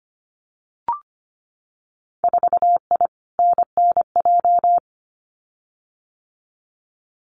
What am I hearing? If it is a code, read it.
Morse code, heard as a keyed tone, about 25 wpm, 700 Hz. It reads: 4I NNJ
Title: Winter Field Day Exchanges 25wpm